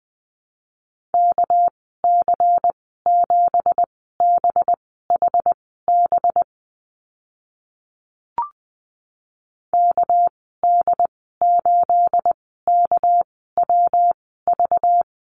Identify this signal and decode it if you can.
Morse code, heard as a keyed tone, about 20 wpm, spaced as for 15 wpm, 700 Hz. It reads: KC7BHB KD8KWV
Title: 219 Lesson 35g - Number 0 - Call Signs 20wpm